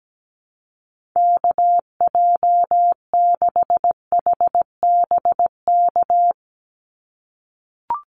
Morse code, heard as a keyed tone, about 17 wpm, 700 Hz. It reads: KJ6HBK
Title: USA Call Signs 17wpm